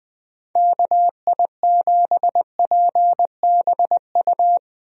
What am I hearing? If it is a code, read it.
Morse code, heard as a keyed tone, about 20 wpm, 700 Hz. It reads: KI7PBU
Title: USA Call Signs - 6 Characters - Repeated 3x - 20wpm